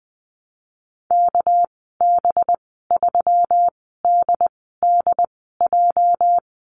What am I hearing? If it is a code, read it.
Morse code, heard as a keyed tone, about 20 wpm, spaced as for 15 wpm, 700 Hz. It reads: KB3DDJ
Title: 231 Lesson 37g - Letter x - Call Signs 20wpm